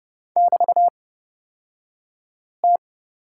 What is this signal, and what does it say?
Morse code, heard as a keyed tone, about 30 wpm, spaced as for 5 wpm, 700 Hz. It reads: <BT>T
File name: Lesson 36d - BT - Two Characters 30wpm